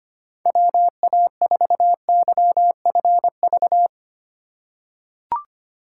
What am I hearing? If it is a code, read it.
Morse code, heard as a keyed tone, about 25 wpm, 700 Hz. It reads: WA4YFV
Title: USA Call Signs - 6 Characters - 4x Character Spacing - 25wpm